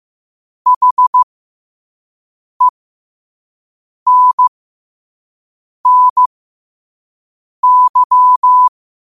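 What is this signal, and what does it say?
Morse code, heard as a keyed tone, about 15 wpm, spaced as for 5 wpm, 1000 Hz. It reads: HENNY
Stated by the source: Morse code practice at 5 wpm.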